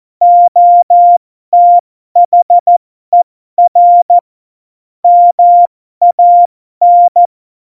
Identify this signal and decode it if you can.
Morse code, heard as a keyed tone, about 14 wpm, spaced as for 12 wpm, 700 Hz. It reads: OTHER MAN
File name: Morsekurs / Witze